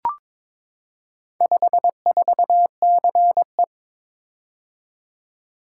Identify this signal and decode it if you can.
Morse code, heard as a keyed tone, about 22 wpm, 700 Hz. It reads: 54CE
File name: Four Letter-Number Combinations - 22wpm